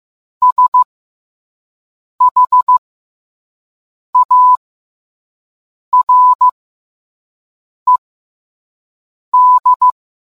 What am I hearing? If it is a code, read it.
Morse code, heard as a keyed tone, about 15 wpm, spaced as for 5 wpm, 1000 Hz. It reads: SHARED